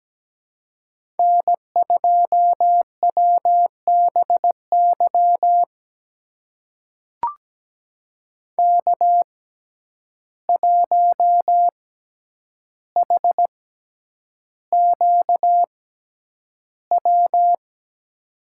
Morse code, heard as a keyed tone, about 17 wpm, spaced as for 9 wpm, 700 Hz. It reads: N2WBY K1HQW
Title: USA Call Signs - 5 Characters - 6x Character Spacing - 17wpm